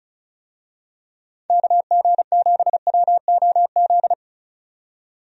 Morse code, heard as a keyed tone, about 36 wpm, 700 Hz. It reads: KG7WOZ
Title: USA Call Signs - 6 Characters - 35wpm